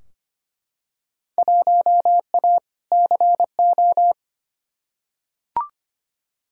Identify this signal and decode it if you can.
Morse code, heard as a keyed tone, about 25 wpm, 700 Hz. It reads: 1A CO
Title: Field Day Exchanges - 25wpm